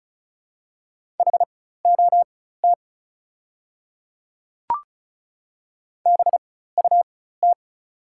Morse code, heard as a keyed tone, about 36 wpm, spaced as for 16 wpm, 690 Hz. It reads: HOT BUT
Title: 4x Character Spacing - Top 100 Words - 35wpm